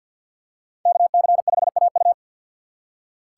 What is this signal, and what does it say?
Morse code, heard as a keyed tone, about 47 wpm, 700 Hz. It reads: KC5AU
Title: USA Call Signs - 5 Characters - 3x Character Spacing - 50wpm